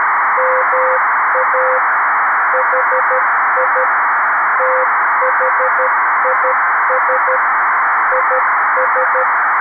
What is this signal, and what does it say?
Morse code, heard as a keyed tone, about 13 wpm, 500 Hz, with noise. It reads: MA HI THIS IS
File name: arissat / recordings / ARISSat 11Sept2011 1354utc 60frames